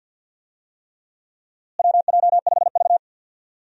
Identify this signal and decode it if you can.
Morse code, heard as a keyed tone, about 47 wpm, 700 Hz. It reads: WJ5V
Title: USA Call Signs - 4 Characters - Repeated 3x - 50wpm